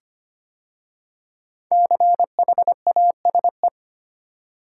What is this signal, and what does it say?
Morse code, heard as a keyed tone, about 25 wpm, 700 Hz. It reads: CHASE
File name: 5 Letter Words - First 3 Letters Given - 25wpm